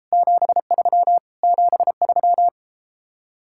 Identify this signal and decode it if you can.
Morse code, heard as a keyed tone, about 33 wpm, 700 Hz. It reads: 73 73